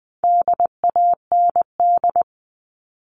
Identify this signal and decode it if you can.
Morse code, heard as a keyed tone, about 20 wpm, 700 Hz. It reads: DAND
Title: All Cut Numbers - 4-Digit 20wpm